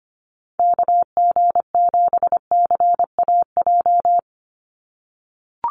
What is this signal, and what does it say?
Morse code, heard as a keyed tone, about 25 wpm, 700 Hz. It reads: KG7CAJ
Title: USA Call Signs 25wpm